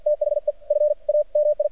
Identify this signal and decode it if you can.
Morse code, heard as a keyed tone, about 53 wpm, spaced as for 44 wpm, 600 Hz. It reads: THE FAMI
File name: archived / CRSnet